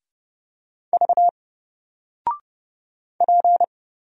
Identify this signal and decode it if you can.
Morse code, heard as a keyed tone, about 30 wpm, 700 Hz. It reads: V P
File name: Single-Letter-Number-30wpm